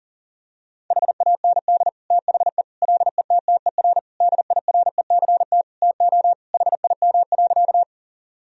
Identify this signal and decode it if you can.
Morse code, heard as a keyed tone, about 40 wpm, 700 Hz. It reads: HAND THE LETTER DIRECT TO HIM.
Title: Sentences from Top 1000 Words - Review Part 2 - 40wpm